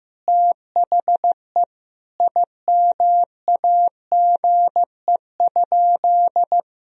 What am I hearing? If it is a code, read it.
Morse code, heard as a keyed tone, about 15 wpm, 700 Hz. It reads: THE IMAGE?
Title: Sentences from Top 1000 Words - Review Part 4 - 15wpm